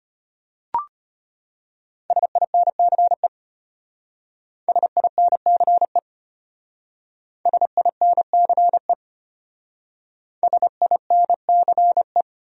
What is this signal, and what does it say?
Morse code, heard as a keyed tone, about 32 wpm, 700 Hz. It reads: SINCE SINCE SINCE SINCE